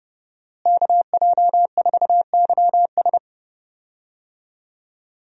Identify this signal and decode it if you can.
Morse code, heard as a keyed tone, about 32 wpm, 700 Hz. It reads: KJ4YS
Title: USA Call Signs - 5 Characters - Repeated 3x - 30wpm